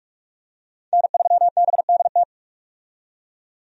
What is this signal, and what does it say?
Morse code, heard as a keyed tone, about 46 wpm, 700 Hz. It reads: N3BDT
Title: USA Call Signs - 5 Characters - 45wpm